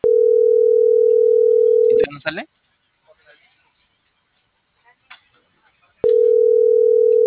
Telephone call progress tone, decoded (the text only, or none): ringback tone